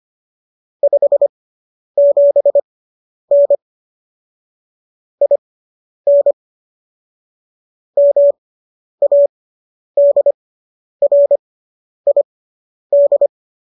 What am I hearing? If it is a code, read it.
Morse code, heard as a keyed tone, about 25 wpm, spaced as for 10 wpm, 570 Hz. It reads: 57N IN MADRID